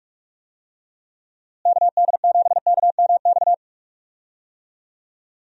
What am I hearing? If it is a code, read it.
Morse code, heard as a keyed tone, about 42 wpm, 700 Hz. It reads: KD7KMX
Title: USA Call Signs - 6 Characters - Repeated 2x - 45wpm